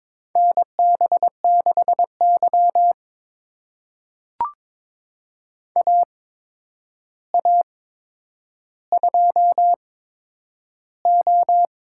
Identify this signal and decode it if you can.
Morse code, heard as a keyed tone, about 22 wpm, spaced as for 9 wpm, 700 Hz. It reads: NB6Y AA2O
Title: USA Call Signs - 4 Characters - 8x Character Spacing - 22wpm